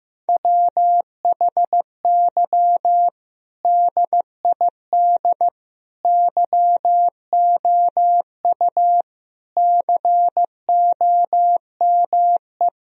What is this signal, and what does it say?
Morse code, heard as a keyed tone, about 16 wpm, 700 Hz. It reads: WHY DID YOU COME
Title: Sentences from Top 1000 Words - Review Part 1 - 15wpm